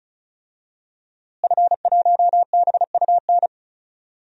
Morse code, heard as a keyed tone, about 36 wpm, 700 Hz. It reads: F1BUN
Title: DXCC-100 Call Signs - 5 Characters - Repeated 3x - 35wpm